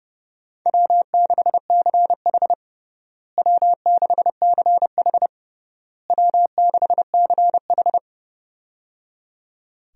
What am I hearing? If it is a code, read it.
Morse code, heard as a keyed tone, about 30 wpm, 700 Hz. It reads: W6CH W6CH W6CH
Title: USA Call Signs - 4 Characters - Repeated 3x - 30wpm